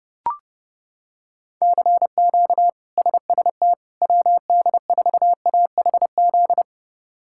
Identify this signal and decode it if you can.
Morse code, heard as a keyed tone, about 30 wpm, 700 Hz. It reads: CQ SST WD4AHZ
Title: K1USN SST QSO Practice - 30wpm